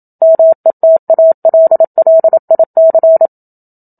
Morse code, heard as a keyed tone, about 27 wpm, 650 Hz. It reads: METALLIC
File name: Book Two / 01-Chapter 1 - 27 WPM